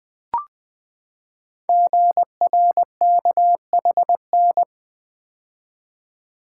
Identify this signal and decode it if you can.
Morse code, heard as a keyed tone, about 21 wpm, 700 Hz. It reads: GRKHN
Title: Code Groups - 5 Letter:Numbers - 20wpm